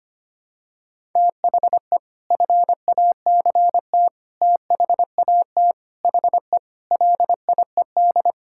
Morse code, heard as a keyed tone, about 25 wpm, 700 Hz. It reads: THE FACT THAT HE LIED
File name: practice / 25wpm / Sentences with 4-Word Phrases - 25wpm